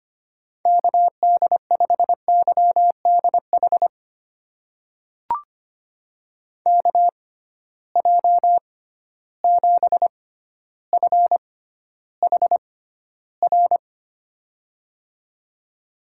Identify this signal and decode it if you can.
Morse code, heard as a keyed tone, about 25 wpm, spaced as for 13 wpm, 700 Hz. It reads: KD5YDH KJ7FHR
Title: USA Call Signs - 6 Characters - 6x Character Spacing - 25wpm